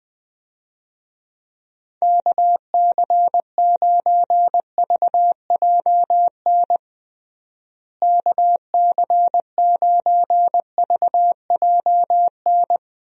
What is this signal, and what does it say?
Morse code, heard as a keyed tone, about 20 wpm, 700 Hz. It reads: KC9VJN KC9VJN
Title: USA Call Signs - 6 Characters - Repeated 2x - 20wpm